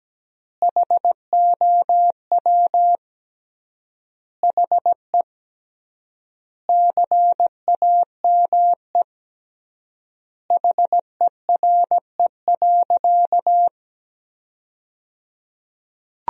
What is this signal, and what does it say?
Morse code, heard as a keyed tone, about 17 wpm, 700 Hz. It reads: HOW HE CAME HERE.
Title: 3x Word-Spacing - Sentences from Top 500 Words - 17wpm